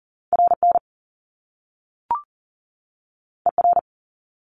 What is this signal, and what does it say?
Morse code, heard as a keyed tone, about 40 wpm, 700 Hz. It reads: RN ER
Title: Top Two Letter Combinations in Words 40wpm